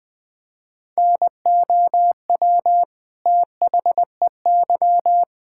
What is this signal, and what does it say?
Morse code, heard as a keyed tone, about 21 wpm, 700 Hz. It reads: NOW THEY
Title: Sentences from Top 1600 Words 20wpm